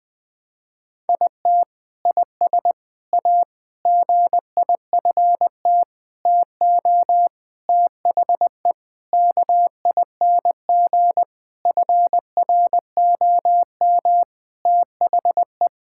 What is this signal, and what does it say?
Morse code, heard as a keyed tone, about 20 wpm, 700 Hz. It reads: IT IS A GIFT TO THE KING FROM THE
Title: Sentences from Top 1000 Words - Review Part 6 - 20wpm